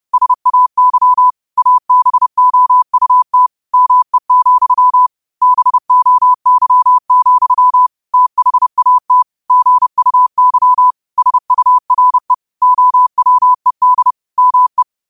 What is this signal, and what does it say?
Morse code, heard as a keyed tone, about 30 wpm, 1000 Hz. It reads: SAY ABOUT ME, BOY, THAT GUY SURE OWED ME